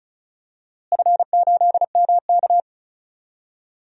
Morse code, heard as a keyed tone, about 36 wpm, 700 Hz. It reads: F8MK
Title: DXCC-100 Call Signs - 4 Characters - Repeated 2x - 35wpm